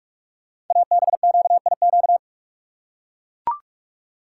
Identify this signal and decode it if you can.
Morse code, heard as a keyed tone, about 44 wpm, 700 Hz. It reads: ABQIQ
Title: Code Groups - 5 Letters - 45wpm